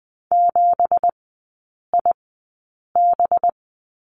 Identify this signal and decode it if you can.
Morse code, heard as a keyed tone, about 20 wpm, spaced as for 8 wpm, 700 Hz. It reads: 7IB
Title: 198 Lesson 32e - Letter k - Three Characters 20wpm